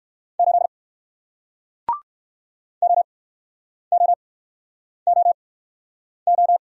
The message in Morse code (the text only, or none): 8 OOOO